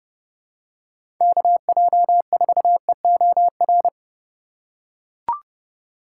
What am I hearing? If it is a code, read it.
Morse code, heard as a keyed tone, about 30 wpm, 700 Hz. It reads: KJ4EOR
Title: USA Call Signs - 6 Characters - 3x Character Spacing - 30wpm